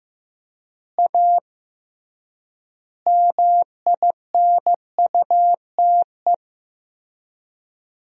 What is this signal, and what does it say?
Morse code, heard as a keyed tone, about 15 wpm, 700 Hz. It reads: A MINUTE
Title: 3x Word-Spacing - Sentences from Top 500 Words - 15wpm